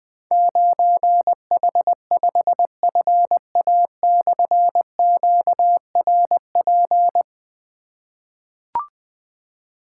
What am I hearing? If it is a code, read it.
Morse code, heard as a keyed tone, about 20 wpm, 700 Hz. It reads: 9H5FA/QRP
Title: Difficult Call Signs 20wpm